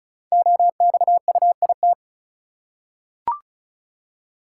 Morse code, heard as a keyed tone, about 36 wpm, 700 Hz. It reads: OXUIT
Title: Code Groups - 5 Letters - 35wpm